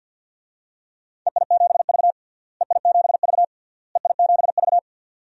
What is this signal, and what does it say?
Morse code, heard as a keyed tone, about 47 wpm, 700 Hz. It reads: EI7V EI7V EI7V
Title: DXCC-100 Call Signs - 4 Characters - Repeated 3x - 50wpm